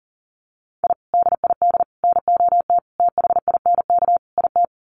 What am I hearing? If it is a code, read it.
Morse code, heard as a keyed tone, about 40 wpm, 700 Hz. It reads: I DID NOT THINK IT
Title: Sets of 5 Words Encore2 40wpm